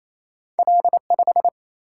Morse code, heard as a keyed tone, about 28 wpm, 700 Hz. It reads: L5
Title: Single-Letter Single-Number 28wpm